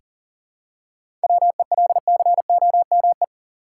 Morse code, heard as a keyed tone, about 40 wpm, 700 Hz. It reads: WELCOME
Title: Seven Letter Words - 40wpm